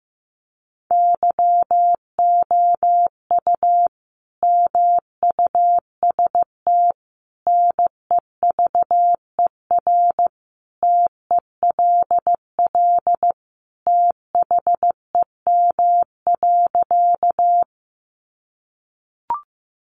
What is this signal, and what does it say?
Morse code, heard as a keyed tone, about 15 wpm, 700 Hz. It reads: YOU MUST NEVER TELL THEM.
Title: Sentences from Top 900 Words 15wpm